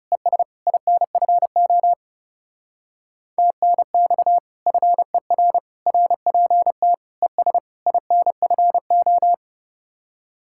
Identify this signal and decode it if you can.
Morse code, heard as a keyed tone, about 36 wpm, 700 Hz. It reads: ES INFO TNX FER RPT ES INFO
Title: Speed-Racing Common QSO Phrases Encore - 30wpm